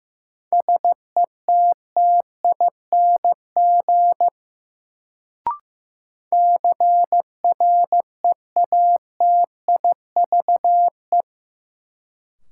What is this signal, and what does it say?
Morse code, heard as a keyed tone, about 15 wpm, 700 Hz. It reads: SETTING CREATIVE
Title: Sentences from Top 1300 Words - New Word Drill Alt - 15wpm